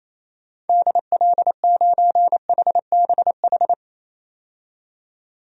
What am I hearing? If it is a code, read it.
Morse code, heard as a keyed tone, about 28 wpm, 700 Hz. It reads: DL9HBH
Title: DXCC-100 Call Signs - 6 Characters - 28wpm